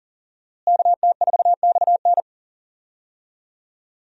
Morse code, heard as a keyed tone, about 40 wpm, 700 Hz. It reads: KT4XN
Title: USA Call Signs - 5 Characters - Repeated 3x - 40wpm